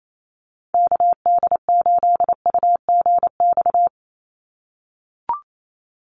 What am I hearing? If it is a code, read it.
Morse code, heard as a keyed tone, about 28 wpm, 700 Hz. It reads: KD8UGX
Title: USA Call Signs 28wpm